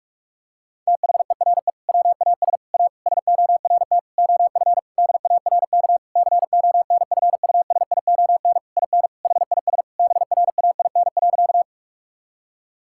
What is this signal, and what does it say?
Morse code, heard as a keyed tone, about 41 wpm, 700 Hz. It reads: THERE WAS A SORT OF DARK CONFUSION IN HIS BRAIN.
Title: Sentences from Top 1600 Words 45wpm